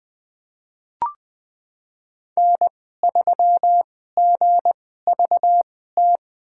Morse code, heard as a keyed tone, about 21 wpm, spaced as for 15 wpm, 700 Hz. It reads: N3GVT
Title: USA Call Signs - 5 Characters - 2x Character Spacing - 20wpm